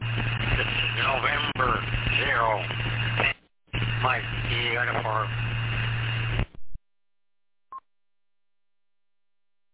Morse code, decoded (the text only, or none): none